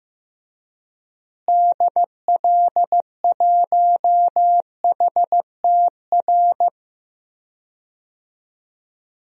Morse code, heard as a keyed tone, about 15 wpm, 700 Hz. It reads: DL1HTR
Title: DXCC-100 Call Signs - 6 Characters - Repeated 2x - 15wpm